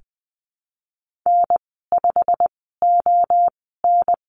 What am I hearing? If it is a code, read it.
Morse code, heard as a keyed tone, about 20 wpm, spaced as for 15 wpm, 700 Hz. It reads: N5ON
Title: 128 Lesson 21g - Number 6 - Call Signs 20wpm